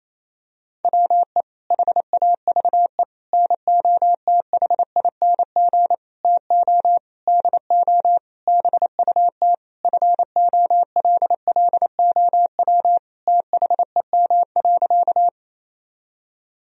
Morse code, heard as a keyed tone, about 28 wpm, 700 Hz. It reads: WE HAVE NOTHING TO DO BUT FOLLOW THEM.
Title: Sentences from Top 1000 Words - Review Part 1 - 28wpm